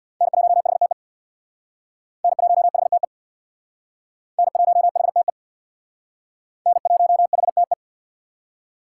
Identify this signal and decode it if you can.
Morse code, heard as a keyed tone, about 59 wpm, 700 Hz. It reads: N1HTE N1HTE N1HTE N1HTE
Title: Speed Racing - USA 1x3 Call Signs 50wpm